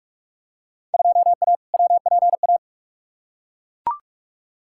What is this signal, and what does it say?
Morse code, heard as a keyed tone, about 42 wpm, 700 Hz. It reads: 2A WPA
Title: Field Day Exchanges - 45wpm